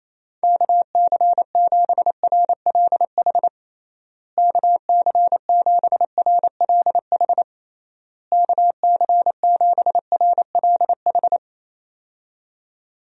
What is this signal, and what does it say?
Morse code, heard as a keyed tone, about 27 wpm, 700 Hz. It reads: KC7RLH KC7RLH KC7RLH
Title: USA Call Signs - 6 Characters - Repeated 3x - 28wpm